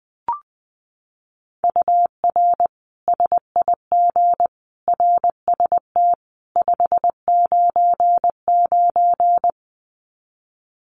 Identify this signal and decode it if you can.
Morse code, heard as a keyed tone, about 20 wpm, 700 Hz. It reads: UR SIG RST 599
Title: Common QSO Phrases 20wpm